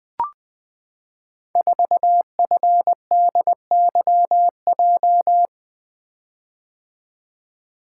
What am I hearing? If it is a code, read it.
Morse code, heard as a keyed tone, about 20 wpm, 700 Hz. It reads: 4FDYJ
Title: Code Groups - 5 Letter:Numbers - 20wpm